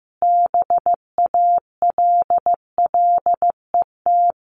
Morse code, heard as a keyed tone, about 15 wpm, 700 Hz. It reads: BALLET
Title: A Sound Of Thunder 15wpm